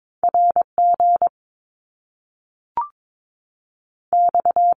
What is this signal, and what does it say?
Morse code, heard as a keyed tone, about 22 wpm, 700 Hz. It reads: R G X
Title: Top Two Letter Combinations in Words 22wpm